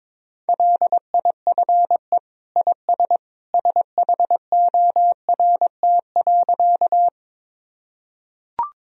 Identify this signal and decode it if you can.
Morse code, heard as a keyed tone, about 22 wpm, 700 Hz. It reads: LIFE IS SHORT.
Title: Sentences from Top 1000 Words - Review Part 2 - 22wpm